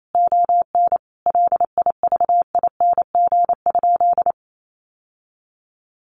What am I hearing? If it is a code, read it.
Morse code, heard as a keyed tone, about 28 wpm, 700 Hz. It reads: ON LIVING?
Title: Sentences from Top 1100 Words 28wpm